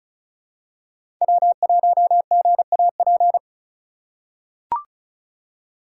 Morse code, heard as a keyed tone, about 36 wpm, 700 Hz. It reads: W1GAP
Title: USA Call Signs - 5 Characters - 3x Character Spacing - 35wpm